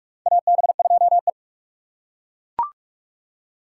Morse code, heard as a keyed tone, about 41 wpm, 700 Hz. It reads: AB2E
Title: USA Call Signs - 2x Character Spacing - 45wpm